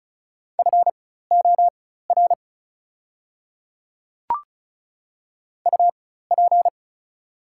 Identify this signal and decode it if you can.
Morse code, heard as a keyed tone, about 34 wpm, spaced as for 16 wpm, 700 Hz. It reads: FOR UP
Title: 4x Character Spacing - Top 100 Words - 35wpm